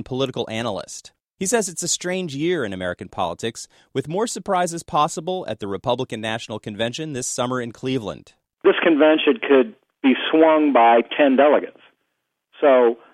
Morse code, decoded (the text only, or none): none